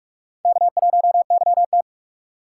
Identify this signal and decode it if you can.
Morse code, heard as a keyed tone, about 46 wpm, 700 Hz. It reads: K1YT
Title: K1USN SST QSO Practice - 45wpm